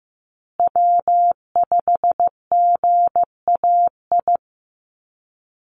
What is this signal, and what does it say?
Morse code, heard as a keyed tone, about 15 wpm, 700 Hz. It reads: W5GAI
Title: Call Signs Repeated 2x - 15wpm